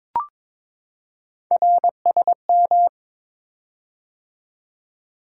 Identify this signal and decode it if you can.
Morse code, heard as a keyed tone, about 22 wpm, 700 Hz. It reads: RSM
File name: practice / 22wpm / Three Letter-Number Combinations - 22wpm